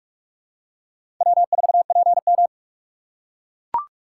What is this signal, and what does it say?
Morse code, heard as a keyed tone, about 44 wpm, 690 Hz. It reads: W4PM
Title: USA Call Signs - 4 Characters - 4x Character Spacing - 45wpm